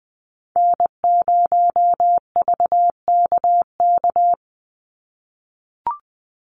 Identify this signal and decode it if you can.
Morse code, heard as a keyed tone, about 20 wpm, 700 Hz. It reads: N0VKK
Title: USA Call Signs 20wpm